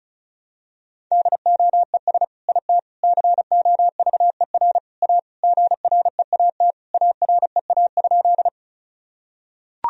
Morse code, heard as a keyed tone, about 34 wpm, 700 Hz. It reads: DOES IT COVER A GREAT AREA?
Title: Sentences from Top 1000 Words - Review Part 2 - 35wpm